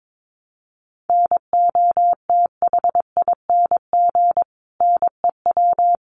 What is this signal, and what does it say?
Morse code, heard as a keyed tone, about 22 wpm, 700 Hz. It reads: NOTHING NEW
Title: Sentences from Top 500 Words 22wpm